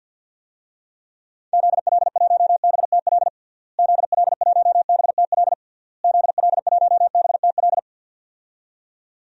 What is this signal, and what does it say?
Morse code, heard as a keyed tone, about 52 wpm, 700 Hz. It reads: ZL1BTL ZL1BTL ZL1BTL
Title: DXCC-100 Call Signs - 6 Characters - Repeated 3x - 50wpm